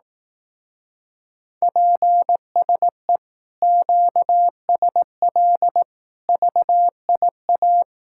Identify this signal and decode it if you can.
Morse code, heard as a keyed tone, about 18 wpm, 700 Hz. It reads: PSE QSL VIA
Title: Speed-Racing Common QSO Phrases Encore - 15wpm